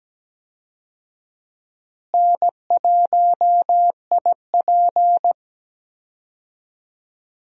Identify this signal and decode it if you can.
Morse code, heard as a keyed tone, about 17 wpm, 700 Hz. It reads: N1IP
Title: USA Call Signs - 4 Characters - Repeated 3x - 17wpm